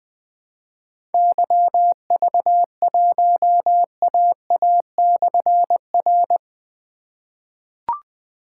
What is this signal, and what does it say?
Morse code, heard as a keyed tone, about 20 wpm, 700 Hz. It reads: YV1AA/R
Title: Difficult Call Signs 20wpm